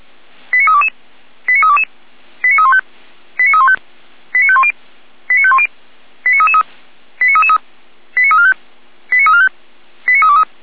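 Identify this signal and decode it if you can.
Five-tone selective call (ZVEI), heard as quick runs of five tones, tones of about 70 ms. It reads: 89322 89322 89315 89315 89510 89510 89333 89333 89345 89345 89323
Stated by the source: ZVEI five-tone call 89015 received with an SDR.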